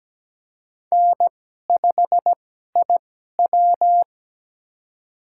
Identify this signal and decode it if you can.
Morse code, heard as a keyed tone, about 17 wpm, spaced as for 12 wpm, 700 Hz. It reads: N5IW
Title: USA Call Signs - 4 Characters - 2x Character Spacing - 17wpm